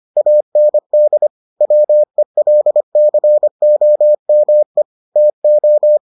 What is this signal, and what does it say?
Morse code, heard as a keyed tone, about 25 wpm, 600 Hz. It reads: AND WELCOME TO